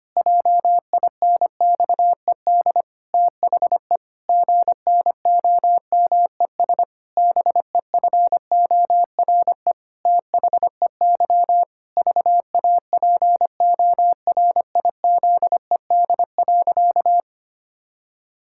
Morse code, heard as a keyed tone, about 25 wpm, 700 Hz. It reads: JINXED THE GNOMES BEFORE THEY VAPORIZED.
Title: Pangram Challenge v2 - 25wpm